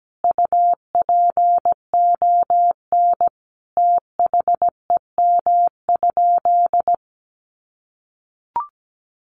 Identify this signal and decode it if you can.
Morse code, heard as a keyed tone, about 17 wpm, 700 Hz. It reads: UPON THEM?
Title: Sentences from Top 1300 Words 17wpm